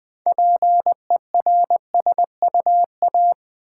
Morse code, heard as a keyed tone, about 20 wpm, 700 Hz. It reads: PERSUA
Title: Build A Word - 3 Letters At A Time - 8-Letter Words - 20wpm